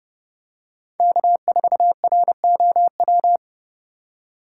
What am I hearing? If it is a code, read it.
Morse code, heard as a keyed tone, about 30 wpm, 700 Hz. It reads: K4ROW